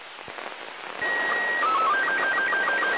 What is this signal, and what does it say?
SSTV (slow-scan television), Robot 36.
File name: ao-51-apollo-11